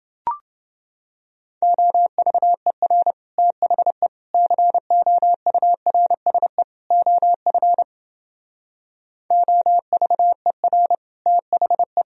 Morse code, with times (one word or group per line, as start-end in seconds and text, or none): 1.6-3.1 OVER
3.4-4.1 THE
4.3-6.6 COURSE
6.9-7.8 OF
9.3-10.9 OVER
11.3-12.0 THE